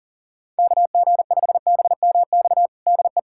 Morse code, heard as a keyed tone, about 40 wpm, 700 Hz. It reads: KG5BMX DE
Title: USA Callsign DE Callsign - 6 Characters - 40wpm